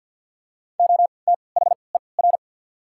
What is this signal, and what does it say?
Morse code, heard as a keyed tone, about 54 wpm, spaced as for 29 wpm, 700 Hz. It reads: OTHER
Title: 3x Character Spacing - Top 100 Words - 50wpm